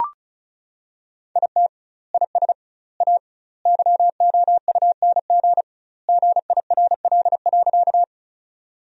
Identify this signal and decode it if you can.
Morse code, heard as a keyed tone, about 36 wpm, 700 Hz. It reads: IT IS A YOUNG GIRL.
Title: 2x Word-Spacing - Sentences from Top 400 Words - 35wpm